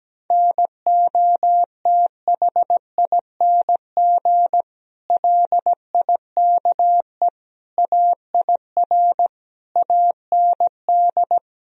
Morse code, heard as a keyed tone, about 17 wpm, 700 Hz. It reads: NOTHING LIKE AIR AND
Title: Sentences from Top 1000 Words - Review Part 4 - 17wpm